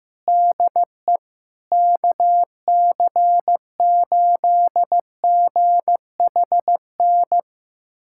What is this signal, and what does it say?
Morse code, heard as a keyed tone, about 15 wpm, 700 Hz. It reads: DE KC8GHN
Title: USA Callsign DE Callsign - 6 Characters - 15wpm